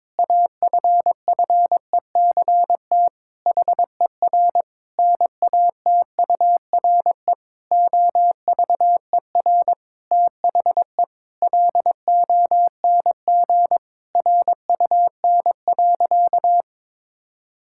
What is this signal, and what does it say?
Morse code, heard as a keyed tone, about 22 wpm, 700 Hz. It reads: AFFECT HER NATURE OVER THE LONG RUN.
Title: Sentences from Top 1000 Words - Review Part 6 - 22wpm